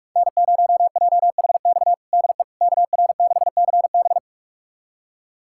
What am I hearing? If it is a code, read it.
Morse code, heard as a keyed tone, about 42 wpm, 700 Hz. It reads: N0JHX DE KR6CB
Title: USA Callsign DE Callsign - 5 Characters - 45wpm